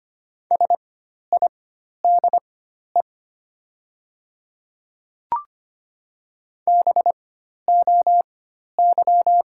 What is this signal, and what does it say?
Morse code, heard as a keyed tone, about 25 wpm, spaced as for 12 wpm, 700 Hz. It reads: SIDE BOY
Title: 4x Character Spacing - Top 200 Words - 25wpm